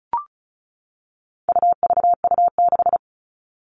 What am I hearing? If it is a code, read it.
Morse code, heard as a keyed tone, about 34 wpm, 700 Hz. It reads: UVU6
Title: All Cut Numbers - 4-Digit 35wpm